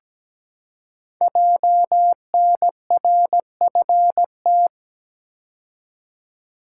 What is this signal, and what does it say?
Morse code, heard as a keyed tone, about 17 wpm, 690 Hz. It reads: JNRFT
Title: Code Groups - 5 Letter:Numbers - 17wpm